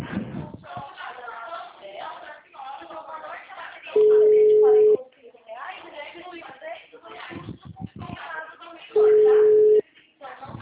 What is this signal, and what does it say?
Ringback tone (Europe, 425 Hz), with speech.